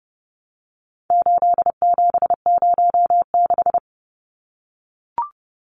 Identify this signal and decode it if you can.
Morse code, heard as a keyed tone, about 30 wpm, 700 Hz. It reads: 8706
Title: Numbers - 4-Digit 30wpm